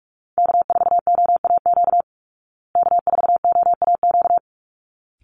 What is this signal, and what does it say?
Morse code, heard as a keyed tone, about 44 wpm, 700 Hz. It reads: K4OAQ K4OAQ
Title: Call Signs Repeated 3x - 45wpm